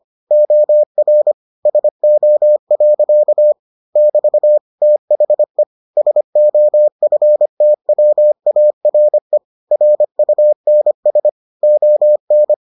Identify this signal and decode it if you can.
Morse code, heard as a keyed tone, about 25 wpm, 600 Hz. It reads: OR SO. <BT> THE SOFTWARE RUNS ON